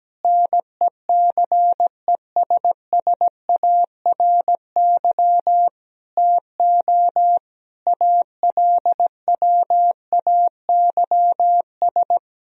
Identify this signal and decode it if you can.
Morse code, heard as a keyed tone, about 17 wpm, 700 Hz. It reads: NECESSARY TO ALWAYS
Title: Sentences from Top 1000 Words - Review Part 4 - 17wpm